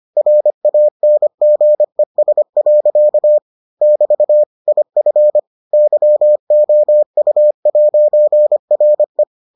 Morse code, heard as a keyed tone, about 25 wpm, 600 Hz. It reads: RANGES. <BT> IF YOU'RE